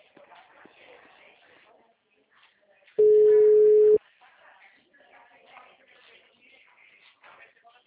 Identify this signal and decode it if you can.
Ringback tone (Europe, 425 Hz), with speech.
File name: out- -2025- - - .30808